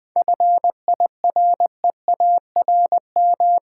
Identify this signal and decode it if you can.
Morse code, heard as a keyed tone, about 20 wpm, 700 Hz. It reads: FIREARM
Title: 7 Letter Words - First 3 Letters Given - 20wpm